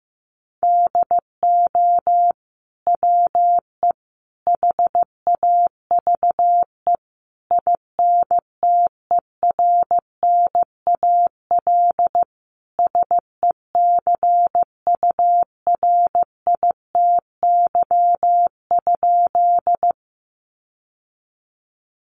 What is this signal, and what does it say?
Morse code, heard as a keyed tone, about 15 wpm, 700 Hz. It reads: DO WE HAVE INTERNAL SECURITY?
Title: Sentences from Top 1400 Words 15wpm